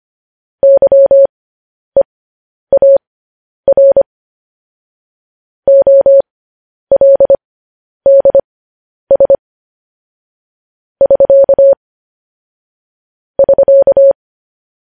Morse code, heard as a keyed tone, about 25 wpm, spaced as for 10 wpm, 550 Hz. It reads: YEAR OLDS <SK> <SK>
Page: News Headlines in Morse Code at 10 WPM